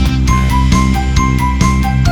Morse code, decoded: none